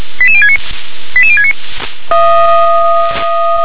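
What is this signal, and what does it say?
Five-tone selective call (ZVEI), heard as quick runs of five tones, tones of about 70 ms. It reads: 88069 88069
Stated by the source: ZVEI five-tone selective call 88769, received with an SDR.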